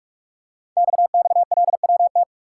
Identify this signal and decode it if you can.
Morse code, heard as a keyed tone, about 44 wpm, 690 Hz. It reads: XXLWT
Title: Code Groups - 5 Letters - 45wpm